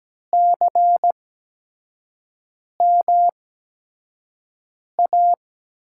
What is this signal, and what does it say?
Morse code, heard as a keyed tone, about 17 wpm, spaced as for 5 wpm, 700 Hz. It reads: CMA
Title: USA Call Signs - 8x Character Spacing - 17wpm